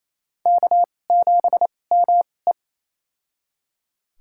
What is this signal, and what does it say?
Morse code, heard as a keyed tone, about 28 wpm, spaced as for 20 wpm, 700 Hz. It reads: K7ME
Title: USA Call Signs - 4 Characters - 2x Character Spacing - 28wpm